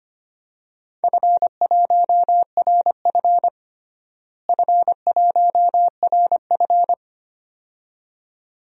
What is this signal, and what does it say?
Morse code, heard as a keyed tone, about 25 wpm, 700 Hz. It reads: F1RF F1RF
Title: DXCC-100 Call Signs - 4 Characters - Repeated 2x - 25wpm